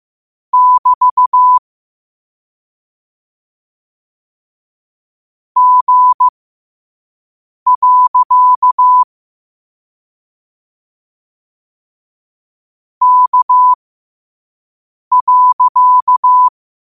Morse code, heard as a keyed tone, about 15 wpm, spaced as for 5 wpm, 1000 Hz. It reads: <BT> G. K.